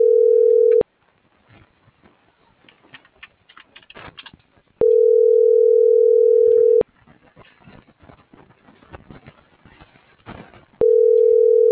Ringback tone (North America), 440 + 480 Hz.